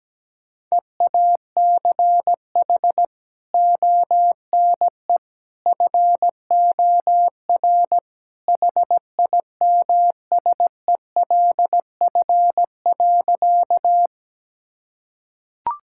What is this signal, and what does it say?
Morse code, heard as a keyed tone, about 17 wpm, 700 Hz. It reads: EACH ONE FOR HIMSELF.
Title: Sentences from Top 1600 Words 17wpm